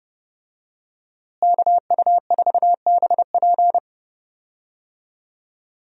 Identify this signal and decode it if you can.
Morse code, heard as a keyed tone, about 32 wpm, 700 Hz. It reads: KU4BP
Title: USA Call Signs - 5 Characters - Repeated 2x - 30wpm